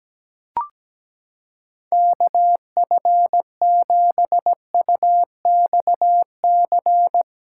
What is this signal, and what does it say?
Morse code, heard as a keyed tone, about 17 wpm, 700 Hz. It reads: KF7UXC
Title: K1USN SST QSO Practice - 17wpm